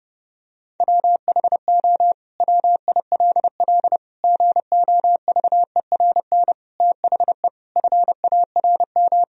Morse code, heard as a keyed tone, about 30 wpm, 700 Hz. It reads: WHO WILL GOVERN THE FARM
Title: Sentences from Top 1000 Words - Review Part 2 - 30wpm